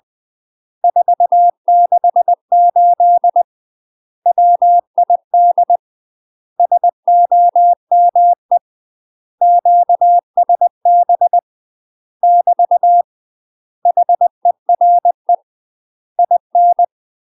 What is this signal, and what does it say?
Morse code, heard as a keyed tone, about 20 wpm, 700 Hz. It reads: 468 WID SOME QSB <BT> HERE IN
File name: UK MASTER QSOs TRAINING / 20WPM / UK MASTER QSOs 0032